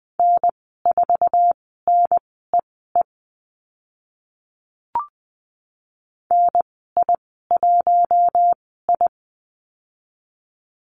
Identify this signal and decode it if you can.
Morse code, heard as a keyed tone, about 20 wpm, spaced as for 15 wpm, 700 Hz. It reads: N4NEE NI1I
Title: 055 Lesson 10g - Letter r - Call Signs 20wpm